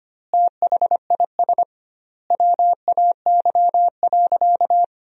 Morse code, heard as a keyed tone, about 25 wpm, 700 Hz. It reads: THIS WAY.
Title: 2x Word-Spacing - Sentences from Top 100 Words - 25wpm